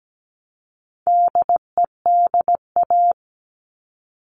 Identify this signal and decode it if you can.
Morse code, heard as a keyed tone, about 17 wpm, 700 Hz. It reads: DEDA
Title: All Cut Numbers - 4-Digit 17wpm